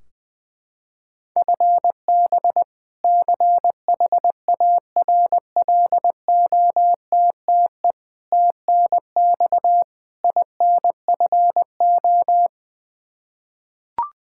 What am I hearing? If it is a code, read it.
Morse code, heard as a keyed tone, about 20 wpm, 700 Hz. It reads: FB CHARLOTTE TNX INFO
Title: Practice QSO - Exchange 3 - More Info - 20wpm